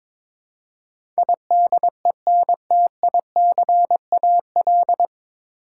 Morse code, heard as a keyed tone, about 22 wpm, 700 Hz. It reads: IDENTICAL
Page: Morse Code Ninja practice 22wpm